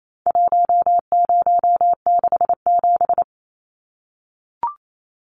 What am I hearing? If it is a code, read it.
Morse code, heard as a keyed tone, about 28 wpm, 700 Hz. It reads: 1067
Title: Numbers - 4-Digit 28wpm